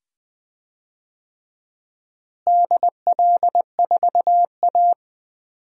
Morse code, heard as a keyed tone, about 20 wpm, 700 Hz. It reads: DL4A
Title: DXCC-100 Call Signs - 4 Characters - Repeated 2x - 20wpm